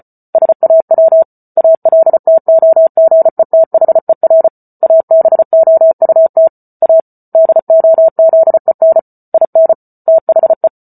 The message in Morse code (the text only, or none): SAW ALTOGETHER ABOUT A DOZEN IN THE